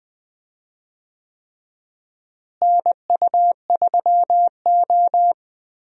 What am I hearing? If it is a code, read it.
Morse code, heard as a keyed tone, about 20 wpm, 700 Hz. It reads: NU3O
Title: USA Call Signs - 4 Characters - Repeated 3x - 20wpm